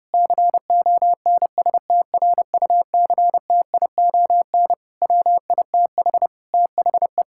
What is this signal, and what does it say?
Morse code, heard as a keyed tone, about 30 wpm, 700 Hz. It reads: CONSTRUCTION WITH THE
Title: Sentences from Top 1600 Words 30wpm